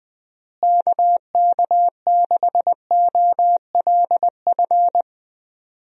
Morse code, heard as a keyed tone, about 21 wpm, 700 Hz. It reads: KK6OLF
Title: USA Call Signs - 6 Characters - Repeated 3x - 20wpm